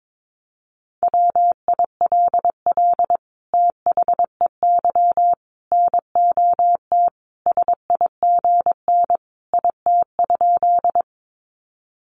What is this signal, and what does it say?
Morse code, heard as a keyed tone, about 22 wpm, 700 Hz. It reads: WILL THEY NOT SIGN IT?
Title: Sentences from Top 1000 Words 22wpm